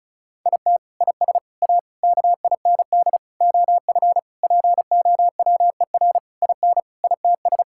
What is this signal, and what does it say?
Morse code, heard as a keyed tone, about 34 wpm, 700 Hz. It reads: IT IS A KIND OF POWER IN ITS
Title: Sentences from Top 1000 Words - Review Part 5 - 35wpm